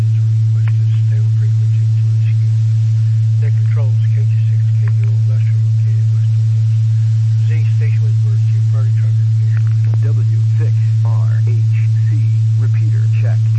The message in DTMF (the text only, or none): none